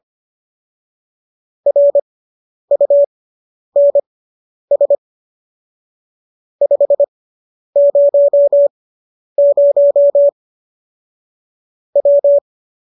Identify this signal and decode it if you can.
Morse code, heard as a keyed tone, about 25 wpm, spaced as for 10 wpm, 570 Hz. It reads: RUNS 500 W